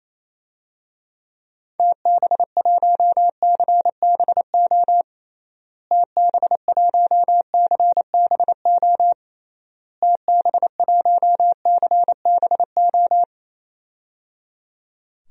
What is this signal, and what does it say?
Morse code, heard as a keyed tone, about 28 wpm, 700 Hz. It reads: TB1CBO TB1CBO TB1CBO
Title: DXCC-100 Call Signs - 6 Characters - Repeated 3x - 28wpm